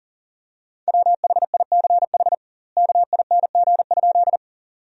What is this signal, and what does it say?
Morse code, heard as a keyed tone, about 40 wpm, 700 Hz. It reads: WHICH KING?
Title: 2x Word-Spacing - Sentences from Top 400 Words - 40wpm